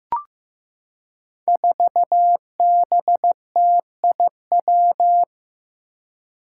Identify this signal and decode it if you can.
Morse code, heard as a keyed tone, about 15 wpm, 700 Hz. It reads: 4BTIW
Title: Five Challenging Letter-Number Combinations 15wpm